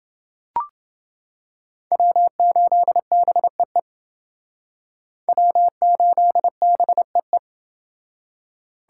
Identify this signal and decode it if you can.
Morse code, heard as a keyed tone, about 29 wpm, 700 Hz. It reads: W8BEE W8BEE